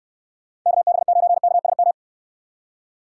Morse code, heard as a keyed tone, about 65 wpm, 700 Hz. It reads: KB9KSM